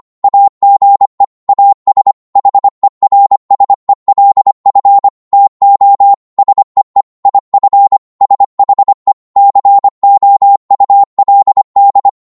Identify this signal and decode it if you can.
Morse code, heard as a keyed tone, about 25 wpm, 800 Hz. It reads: AGE AS HERSELF TO SEE IF SHE COULD